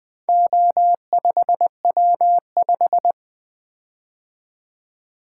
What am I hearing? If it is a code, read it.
Morse code, heard as a keyed tone, about 20 wpm, 700 Hz. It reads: O5W5
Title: Four Letter-Number Combinations - 20wpm